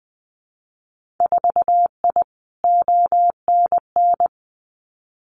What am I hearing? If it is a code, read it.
Morse code, heard as a keyed tone, about 20 wpm, 700 Hz. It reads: 4I ONN
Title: Winter Field Day Exchanges Encore 20wpm